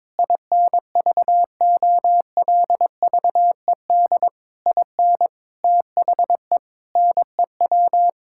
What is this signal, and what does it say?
Morse code, heard as a keyed tone, about 22 wpm, 700 Hz. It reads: INVOLVED IN THE NEW
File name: practice / 22wpm / Sets of 8 Words - 22wpm